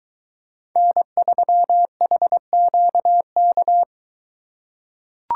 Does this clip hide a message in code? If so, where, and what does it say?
Yes: Morse code from 0.8 to 3.8 s, heard as a keyed tone, about 23 wpm, 700 Hz. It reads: N3HQK